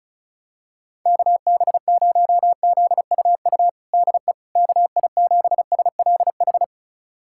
Morse code, heard as a keyed tone, about 36 wpm, 700 Hz. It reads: KB0ZUU DE KI7SLH
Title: USA Callsign DE Callsign - 6 Characters - 35wpm